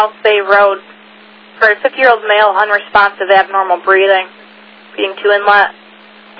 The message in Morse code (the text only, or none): none